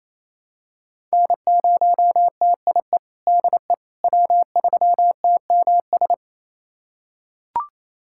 Morse code, heard as a keyed tone, about 28 wpm, 700 Hz. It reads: N0TIE DE W3TMS
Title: USA Callsign DE Callsign - 5 Characters - 28wpm